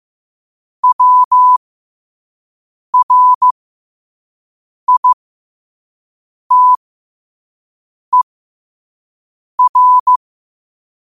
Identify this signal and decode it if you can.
Morse code, heard as a keyed tone, about 15 wpm, spaced as for 5 wpm, 1000 Hz. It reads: WRITER
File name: qotdfiles / 5wpm